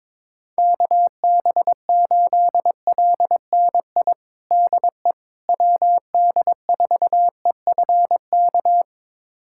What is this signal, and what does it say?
Morse code, heard as a keyed tone, about 22 wpm, 700 Hz. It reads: KB8LNI DE WD4EFK
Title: USA Callsign DE Callsign 22wpm